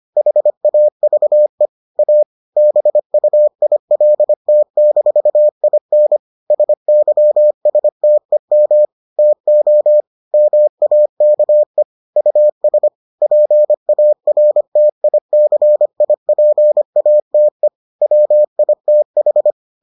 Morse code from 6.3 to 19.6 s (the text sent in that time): SYSTEM TO MAKE US PARTICIPATE WITH